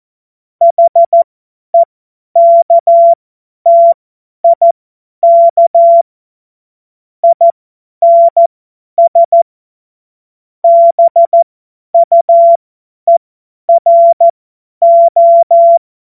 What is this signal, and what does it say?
Morse code, heard as a keyed tone, about 14 wpm, spaced as for 10 wpm, 680 Hz. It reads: HEKTIK INS BUERO
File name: Morsekurs / Witze / Witz 02. Mai 2023 50 F70